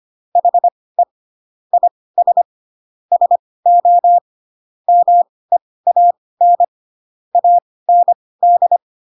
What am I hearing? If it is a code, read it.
Morse code, heard as a keyed tone, about 25 wpm, spaced as for 18 wpm, 700 Hz. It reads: HE IS SO MEAN AND
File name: Pinocchio ch3 12 25 ik1hge